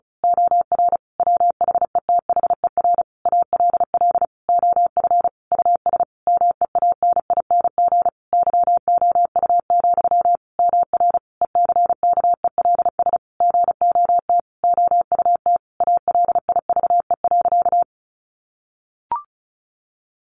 Morse code, heard as a keyed tone, about 34 wpm, 700 Hz. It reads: OR WHETHER ALL OF US MEANING YOU, MR ECKELS GOT OUT ALIVE.
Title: A Sound Of Thunder 35wpm